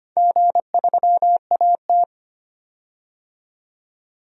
Morse code, heard as a keyed tone, about 25 wpm, 700 Hz. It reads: G3AT
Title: DXCC-100 Call Signs - 4 Characters - 25wpm